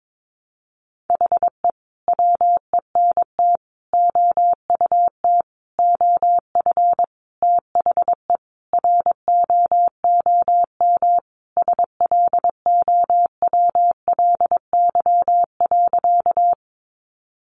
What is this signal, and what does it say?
Morse code, heard as a keyed tone, about 22 wpm, 700 Hz. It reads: HE WENT OUT OF THE ROOM SLOWLY.
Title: Sentences from Top 1100 Words 22wpm